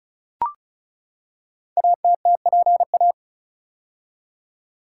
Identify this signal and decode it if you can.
Morse code, heard as a keyed tone, about 36 wpm, 700 Hz. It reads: ATTPA
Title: Code Groups - 5 Letters - 35wpm